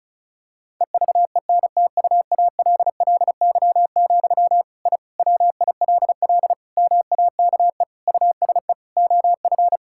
Morse code, heard as a keyed tone, about 36 wpm, 700 Hz. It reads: EVENTUALLY, I WILL MAKE USE OF
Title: Sentences from Top 1000 Words - Review Part 6 - 35wpm